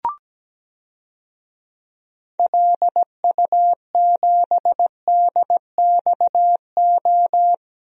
Morse code, heard as a keyed tone, about 17 wpm, 700 Hz. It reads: LU7DXO
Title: DXCC-100 Call Signs - 6 Characters - Repeated 2x - 17wpm